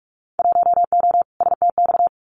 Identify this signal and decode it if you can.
Morse code, heard as a keyed tone, about 44 wpm, 700 Hz. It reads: 1O STX
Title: Winter Field Day Exchanges 45wpm